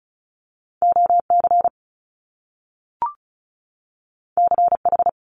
Morse code, heard as a keyed tone, about 36 wpm, 700 Hz. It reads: OC CH
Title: Top Two Letter Combinations in Words 35wpm